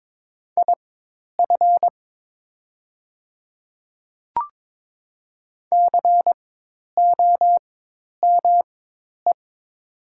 Morse code, heard as a keyed tone, about 22 wpm, spaced as for 10 wpm, 700 Hz. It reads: IF COME
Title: 4x Character Spacing - Top 100 Words - 22wpm